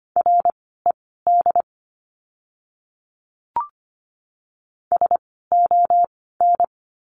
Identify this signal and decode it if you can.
Morse code, heard as a keyed tone, about 25 wpm, spaced as for 16 wpm, 700 Hz. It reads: RED SON
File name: course / 25:10wpm / 094 Lesson 16f - Letter c - Words 25wpm